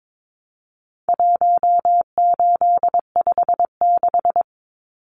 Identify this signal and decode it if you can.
Morse code, heard as a keyed tone, about 22 wpm, 700 Hz. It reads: 1856
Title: Numbers - 4-Digit 22wpm